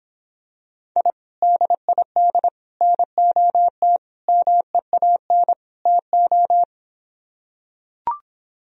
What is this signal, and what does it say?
Morse code, heard as a keyed tone, about 26 wpm, 700 Hz. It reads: I DID NOT MEAN TO